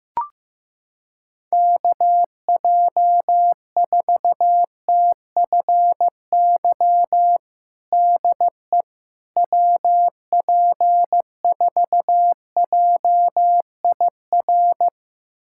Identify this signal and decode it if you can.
Morse code, heard as a keyed tone, about 15 wpm, 700 Hz. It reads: KJ4TFY DE WP4JIR
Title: USA Callsign DE Callsign 15wpm